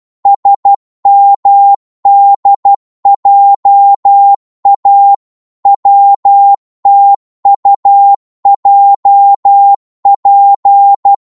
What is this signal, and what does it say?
Morse code, heard as a keyed tone, about 12 wpm, 800 Hz. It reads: SMDJA WTUJP
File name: Morse / ltr-12wpm